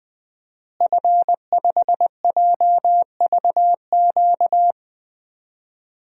Morse code, heard as a keyed tone, about 20 wpm, 700 Hz. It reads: F5JVQ
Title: DXCC-100 Call Signs - 5 Characters - Repeated 2x - 20wpm